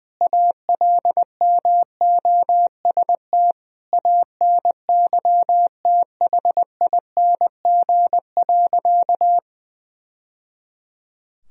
Morse code, heard as a keyed tone, about 20 wpm, 700 Hz. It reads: ALMOST ANYTHING.
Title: Sentences from Top 1000 Words - Review Part 4 - 20wpm